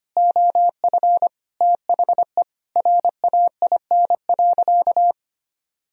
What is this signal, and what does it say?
Morse code, heard as a keyed tone, about 25 wpm, 700 Hz. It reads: OF THE RAIN.
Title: Sentences from Top 1000 Words - Review Part 1 - 25wpm